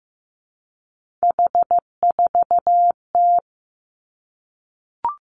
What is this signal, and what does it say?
Morse code, heard as a keyed tone, about 15 wpm, 700 Hz. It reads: H4T
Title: Three Challenging Letter-Number Combinations 15wpm